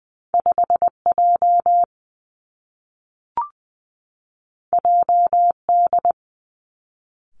Morse code, heard as a keyed tone, about 20 wpm, 700 Hz. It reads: 5 J J D
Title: Two Challenging Letter-Number Combinations 20wpm